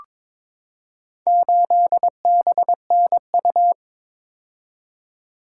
Morse code, heard as a keyed tone, about 22 wpm, 700 Hz. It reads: 8BNU
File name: practice / 22wpm / Four Letter-Number Combinations - 22wpm